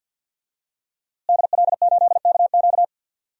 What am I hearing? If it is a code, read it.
Morse code, heard as a keyed tone, about 52 wpm, 700 Hz. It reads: DL8KQ
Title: DXCC-100 Call Signs - 5 Characters - Repeated 3x - 50wpm